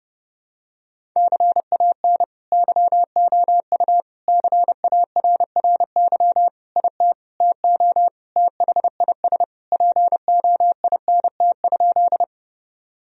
Morse code, heard as a keyed tone, about 30 wpm, 700 Hz. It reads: CAN YOU CARRY IT TO THIS POINT?
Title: Sentences from Top 1000 Words - Review Part 1 - 30wpm